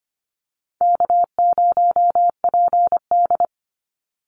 Morse code, heard as a keyed tone, about 25 wpm, 700 Hz. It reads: K0PD
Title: Easy Call Signs Repeated 3x - 25wpm